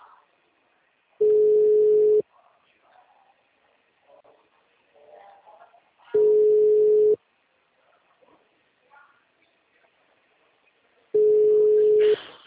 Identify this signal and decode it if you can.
Ringback tone (Europe, 425 Hz).
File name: out- -2012